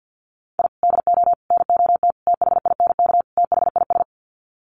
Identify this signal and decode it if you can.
Morse code, heard as a keyed tone, about 49 wpm, 700 Hz. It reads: I DO NOT THINK THIS
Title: Sets of 5 Words Encore 50wpm